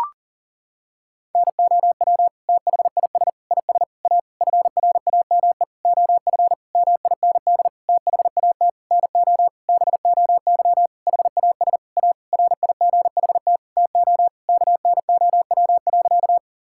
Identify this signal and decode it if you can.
Morse code, heard as a keyed tone, about 40 wpm, 700 Hz. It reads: NOW THIS IS A FRAME OF MIND THAT NO BOY HAS A RIGHT TO KNOW.
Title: Sentences from Top 1000 Words - Review Part 6 - 40wpm